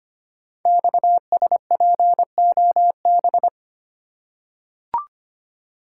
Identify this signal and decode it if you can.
Morse code, heard as a keyed tone, about 25 wpm, 700 Hz. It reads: XSPOB
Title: Code Groups - 5 Letters - 25wpm